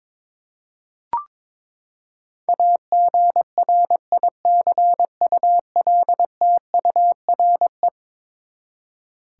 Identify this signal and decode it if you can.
Morse code, heard as a keyed tone, about 22 wpm, 700 Hz. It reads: AGRICULTURE
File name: practice / 22wpm / Eleven Letter Words - 22wpm